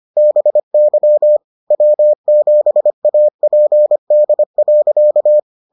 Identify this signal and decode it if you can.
Morse code, heard as a keyed tone, about 25 wpm, 600 Hz. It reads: BY W7APD.